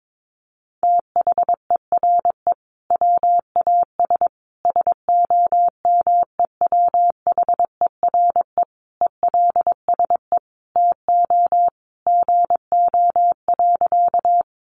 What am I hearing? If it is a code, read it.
Morse code, heard as a keyed tone, about 22 wpm, 700 Hz. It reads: THERE WAS SOMEWHERE ELSE TO GO.
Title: Sentences from Top 1400 Words 22wpm